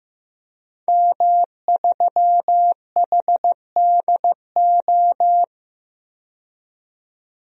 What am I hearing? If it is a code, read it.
Morse code, heard as a keyed tone, about 15 wpm, 700 Hz. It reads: M3HDO
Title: DXCC-100 Call Signs - 5 Characters - 15wpm